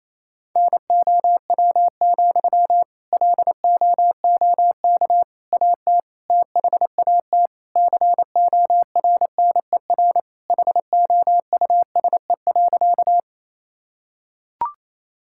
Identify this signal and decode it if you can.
Morse code, heard as a keyed tone, about 28 wpm, 700 Hz. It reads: NOW, LOOK AT THAT CORNER HOUSE.
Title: Sentences from Top 1600 Words 28wpm